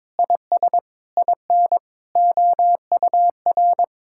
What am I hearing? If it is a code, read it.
Morse code, heard as a keyed tone, about 22 wpm, 700 Hz. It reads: IS IN OUR
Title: Sentences from Top 1000 Words - Review Part 5 - 22wpm